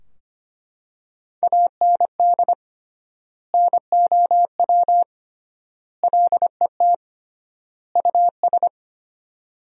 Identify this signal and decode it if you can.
Morse code, heard as a keyed tone, about 25 wpm, 700 Hz. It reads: AND NOW LET US
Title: 3x Word-Spacing - Sentences from Top 300 Words - 25wpm